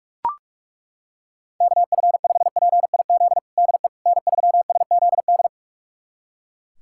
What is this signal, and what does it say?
Morse code, heard as a keyed tone, about 41 wpm, 700 Hz. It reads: KF5PIZ DE N3SZD
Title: USA Callsign DE Callsign 45wpm